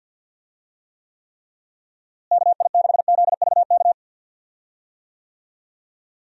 Morse code, heard as a keyed tone, about 50 wpm, 700 Hz. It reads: KI6ZVK
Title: USA Call Signs - 6 Characters - Repeated 3x - 50wpm